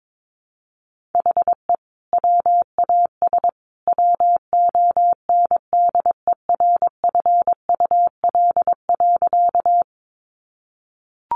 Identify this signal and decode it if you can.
Morse code, heard as a keyed tone, about 22 wpm, 700 Hz. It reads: HE WAS WONDERFUL.
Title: Sentences from Top 1100 Words 22wpm